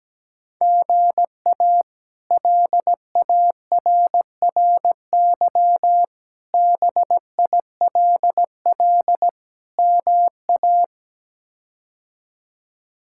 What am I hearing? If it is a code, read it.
Morse code, heard as a keyed tone, about 17 wpm, 700 Hz. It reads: GA LARRY BILL MA
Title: K1USN SST QSO Practice - 17wpm